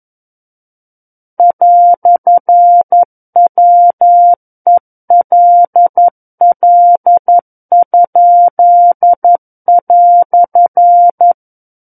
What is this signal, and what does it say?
Morse code, heard as a keyed tone, about 11 wpm, 700 Hz. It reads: "WELL?"
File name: 11 WPM